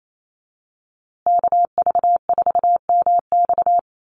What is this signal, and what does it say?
Morse code, heard as a keyed tone, about 28 wpm, 700 Hz. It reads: KV4MX
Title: USA Call Signs Repeated 3x 28wpm